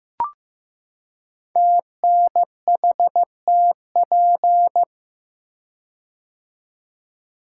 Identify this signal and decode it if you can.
Morse code, heard as a keyed tone, about 15 wpm, 700 Hz. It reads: TNHTP
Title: Code Groups - 5 Letters - 15wpm